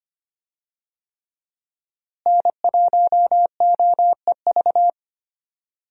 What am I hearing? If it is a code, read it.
Morse code, heard as a keyed tone, about 25 wpm, 700 Hz. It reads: N1OEV
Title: USA Call Signs - 5 Characters - Repeated 3x - 25wpm